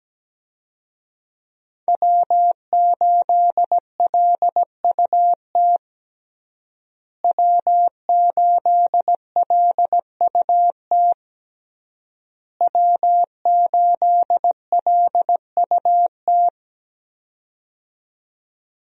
Morse code, heard as a keyed tone, about 17 wpm, 700 Hz. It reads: W8LUT W8LUT W8LUT
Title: USA Call Signs - 5 Characters - Repeated 3x - 17wpm